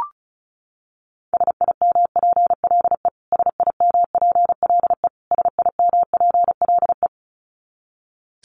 Morse code, heard as a keyed tone, about 34 wpm, 700 Hz. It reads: SIMPLE SIMPLE SIMPLE
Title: Top 500 Words Repeated 35wpm